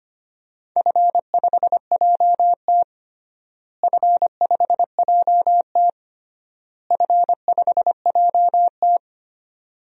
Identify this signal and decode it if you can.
Morse code, heard as a keyed tone, about 25 wpm, 700 Hz. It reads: F5JT F5JT F5JT
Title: DXCC-100 Call Signs - 4 Characters - Repeated 3x - 25wpm